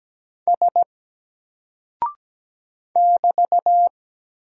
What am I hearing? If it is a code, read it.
Morse code, heard as a keyed tone, about 17 wpm, 700 Hz. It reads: S <BT>
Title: Single Character (Letter:Number:Punctuation:Prosign) 17wpm